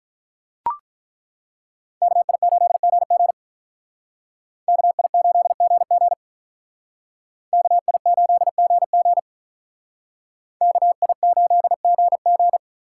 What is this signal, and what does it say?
Morse code, heard as a keyed tone, about 42 wpm, 700 Hz. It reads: KI8GG KI8GG KI8GG KI8GG
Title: Speed Racing - USA 2x2 Call Signs 35wpm